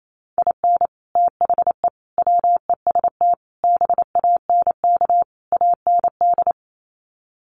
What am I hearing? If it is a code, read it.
Morse code, heard as a keyed tone, about 28 wpm, 700 Hz. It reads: IN THE WEST BANK AND
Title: Sets of 5 Words Encore2 28wpm